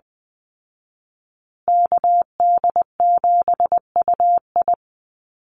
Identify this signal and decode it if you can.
Morse code, heard as a keyed tone, about 20 wpm, 700 Hz. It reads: KD7UI
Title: Call Signs Repeated 2x - 20wpm